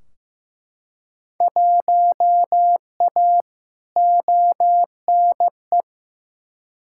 Morse code, heard as a keyed tone, about 15 wpm, 700 Hz. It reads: 1A ONE
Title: Field Day Exchanges - 15wpm